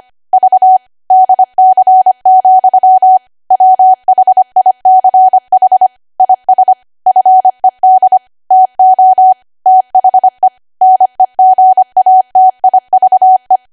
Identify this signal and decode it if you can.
Morse code, heard as a keyed tone, about 25 wpm, 750 Hz. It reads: V DC, WHICH IS FED TO THE NEGATIVE
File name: Archive / 25 WPM / 25WPM